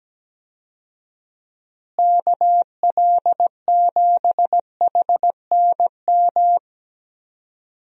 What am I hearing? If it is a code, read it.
Morse code, heard as a keyed tone, about 17 wpm, 700 Hz. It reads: KL7HNM
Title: USA Call Signs - 6 Characters - Repeated 3x - 17wpm